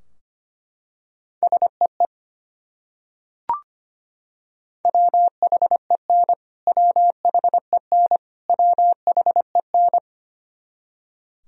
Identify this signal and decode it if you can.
Morse code, heard as a keyed tone, about 25 wpm, 700 Hz. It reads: SEE WHEN WHEN WHEN
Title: Top 100 Words Repeated 25wpm